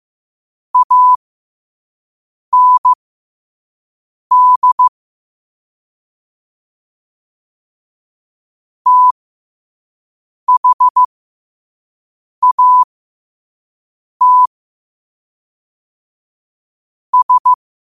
Morse code, heard as a keyed tone, about 15 wpm, spaced as for 5 wpm, 1000 Hz. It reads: AND THAT S
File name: 5wpm